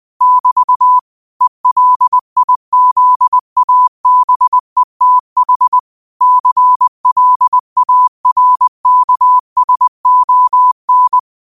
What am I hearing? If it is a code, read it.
Morse code, heard as a keyed tone, about 20 wpm, 1000 Hz. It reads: <BT> ELIZABETH CLARKSON
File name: qotdfiles / 20wpm